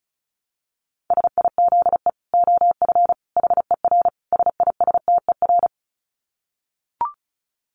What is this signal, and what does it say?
Morse code, heard as a keyed tone, about 34 wpm, 700 Hz. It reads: SIZE OF HER SISTER
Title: Dit Challenge 35wpm